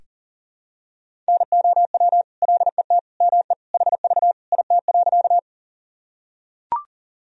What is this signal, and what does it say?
Morse code, heard as a keyed tone, about 40 wpm, 700 Hz. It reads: NOW LET ME HV IT.
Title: Sentences from Top 300 Words and Abbreviations 40wpm